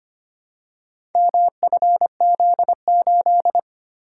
Morse code, heard as a keyed tone, about 25 wpm, 700 Hz. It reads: MFZ8
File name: practice / 25wpm / Four Letter-Number Combinations - 25wpm